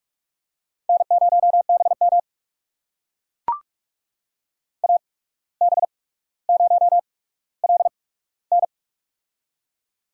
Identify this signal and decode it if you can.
Morse code, heard as a keyed tone, about 44 wpm, spaced as for 18 wpm, 700 Hz. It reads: N0BM AB0LN
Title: USA Call Signs - 8x Character Spacing - 45wpm